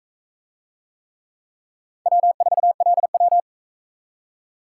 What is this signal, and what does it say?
Morse code, heard as a keyed tone, about 41 wpm, 700 Hz. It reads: W4LW